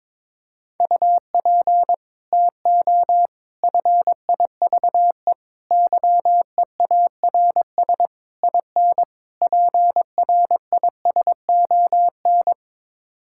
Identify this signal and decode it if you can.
Morse code, heard as a keyed tone, about 22 wpm, 700 Hz. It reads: UP TO FIVE YEARS IN PRISON
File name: practice / 22wpm / Sets of 6 Words - 22wpm